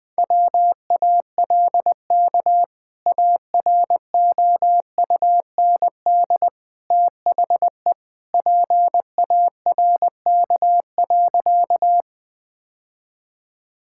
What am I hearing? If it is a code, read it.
Morse code, heard as a keyed tone, about 20 wpm, 700 Hz. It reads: WALK AROUND THE PARK.
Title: Sentences with 2-Word Phrases - 20wpm